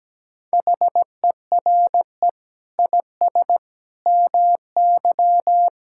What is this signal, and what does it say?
Morse code, heard as a keyed tone, about 17 wpm, 700 Hz. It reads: HERE IS MY
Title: Sentences from Top 1600 Words 17wpm